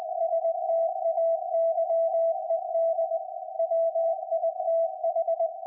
Morse code, heard as a keyed tone, about 20 wpm, 660 Hz. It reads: STAYED WITH
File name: IC-7100